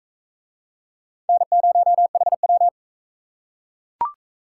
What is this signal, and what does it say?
Morse code, heard as a keyed tone, about 35 wpm, 700 Hz. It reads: N0HW